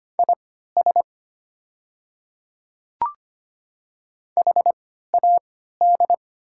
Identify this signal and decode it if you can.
Morse code, heard as a keyed tone, about 25 wpm, spaced as for 14 wpm, 700 Hz. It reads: IS HAD
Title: 3x Character Spacing - Top 100 Words - 25wpm